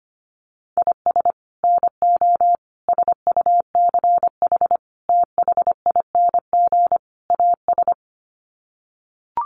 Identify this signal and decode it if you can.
Morse code, heard as a keyed tone, about 25 wpm, 700 Hz. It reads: IS NO SUCH THING AS
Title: Sets of 5 Words Encore 25wpm